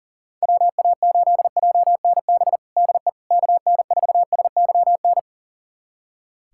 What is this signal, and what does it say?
Morse code, heard as a keyed tone, about 40 wpm, 700 Hz. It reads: WA8JNB DE KN4SYN
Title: USA Callsign DE Callsign - 6 Characters - 40wpm